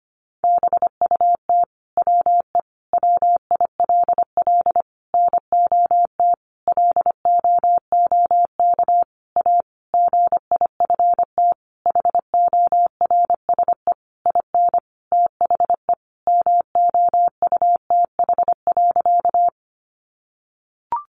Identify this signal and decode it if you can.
Morse code, heard as a keyed tone, about 25 wpm, 700 Hz. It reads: BUT WE WILL NOT LOOK A GIFT HORSE IN THE MOUTH.
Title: Sentences from Top 1500 Words 25wpm